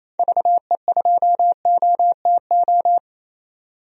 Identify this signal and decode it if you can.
Morse code, heard as a keyed tone, about 28 wpm, 700 Hz. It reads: VE2OTO
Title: DXCC-100 Call Signs - 6 Characters - 28wpm